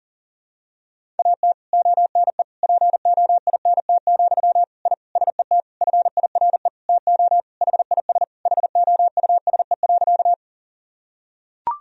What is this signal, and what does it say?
Morse code, heard as a keyed tone, about 37 wpm, 700 Hz. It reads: AT ONE POINT, I SET FIRE TO HIS HOUSE.
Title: Sentences from Top 1000 Words - Review Part 1 - 40wpm